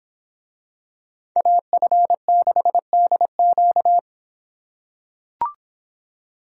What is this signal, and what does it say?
Morse code, heard as a keyed tone, about 24 wpm, 700 Hz. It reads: AF6DQ